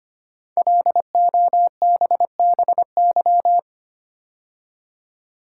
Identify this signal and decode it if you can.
Morse code, heard as a keyed tone, about 25 wpm, 700 Hz. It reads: LOBBY
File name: 5 Letter Words - First 3 Letters Given - 25wpm